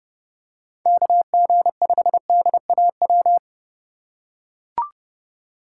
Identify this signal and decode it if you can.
Morse code, heard as a keyed tone, about 30 wpm, 700 Hz. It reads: KG5DAW